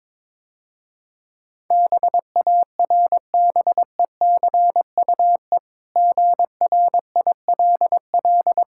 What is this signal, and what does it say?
Morse code, heard as a keyed tone, about 22 wpm, 700 Hz. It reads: BARBECUE GRILL
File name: Easy Words - Speed-Racing with Hints - 22wpm